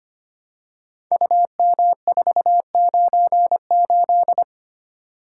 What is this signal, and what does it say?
Morse code, heard as a keyed tone, about 25 wpm, 700 Hz. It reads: UM498
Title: Code Groups - 5 Letter:Numbers - 25wpm